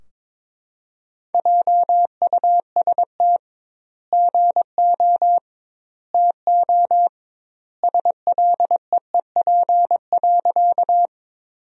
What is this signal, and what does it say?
Morse code, heard as a keyed tone, about 22 wpm, 700 Hz. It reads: JUST GO TO SLEEP.
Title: 2x Word-Spacing - Sentences from Top 500 Words - 22wpm